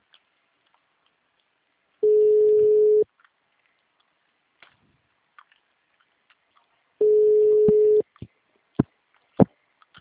Ringback tone (Europe, 425 Hz).
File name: out- -2003- - - .625